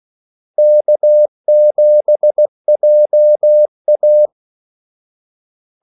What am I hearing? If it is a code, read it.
Morse code, heard as a keyed tone, about 16 wpm, 600 Hz. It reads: K7JA
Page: Kaldesignaler 16 wpm